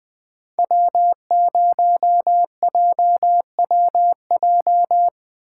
Morse code, heard as a keyed tone, about 20 wpm, 700 Hz. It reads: W0JWJ
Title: Speed Racing - USA 1x3 Call Signs 17wpm